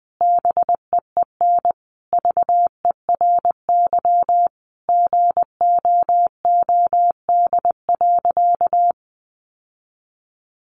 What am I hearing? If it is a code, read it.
Morse code, heard as a keyed tone, about 20 wpm, 700 Hz. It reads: BEEN VERY GOOD.
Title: Sentences from Top 1400 Words 20wpm